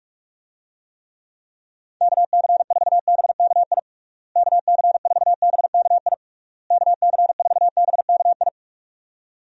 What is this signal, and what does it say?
Morse code, heard as a keyed tone, about 41 wpm, 700 Hz. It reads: KC4BKI KC4BKI KC4BKI
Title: USA Call Signs - 6 Characters - Repeated 3x - 45wpm